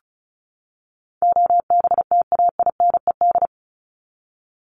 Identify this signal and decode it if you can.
Morse code, heard as a keyed tone, about 34 wpm, 700 Hz. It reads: OBTAINED
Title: ED Suffix - Top 100 Words with Base Words 35wpm